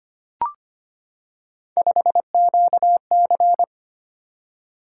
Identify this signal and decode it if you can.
Morse code, heard as a keyed tone, about 25 wpm, 700 Hz. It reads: 5QC
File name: Three Letter-Number Combinations - 25wpm